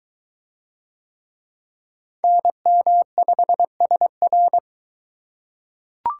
Morse code, heard as a keyed tone, about 22 wpm, 700 Hz. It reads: NM5SR